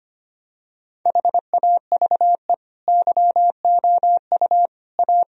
Morse code, heard as a keyed tone, about 25 wpm, 700 Hz. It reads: HAVE YOU A
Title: Sentences from Top 1000 Words - Review Part 1 - 25wpm